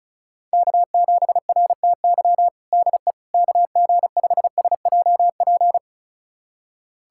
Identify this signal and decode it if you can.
Morse code, heard as a keyed tone, about 34 wpm, 700 Hz. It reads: K7RTY DE KG5SJP
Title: USA Callsign DE Callsign 35wpm